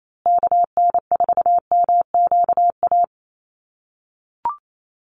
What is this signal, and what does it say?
Morse code, heard as a keyed tone, about 28 wpm, 690 Hz. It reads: KN4MQA
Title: USA Call Signs Repeated 3x 28wpm